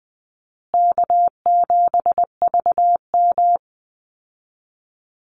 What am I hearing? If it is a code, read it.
Morse code, heard as a keyed tone, about 20 wpm, 700 Hz. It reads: K7VM
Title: Call Signs Repeated 2x - 20wpm